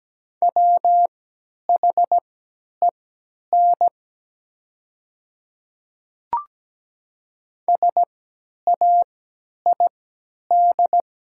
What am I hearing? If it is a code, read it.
Morse code, heard as a keyed tone, about 17 wpm, spaced as for 10 wpm, 700 Hz. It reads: WHEN SAID
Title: 3x Character Spacing - Top 200 Words - 17wpm